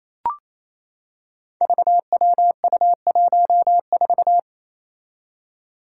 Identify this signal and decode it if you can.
Morse code, heard as a keyed tone, about 28 wpm, 700 Hz. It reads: VWU14
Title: Five Challenging Letter-Number Combinations 28wpm